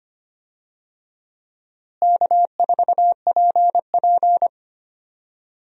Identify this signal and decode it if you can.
Morse code, heard as a keyed tone, about 25 wpm, 700 Hz. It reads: K4PP
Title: USA Call Signs - 4 Characters - Repeated 3x - 25wpm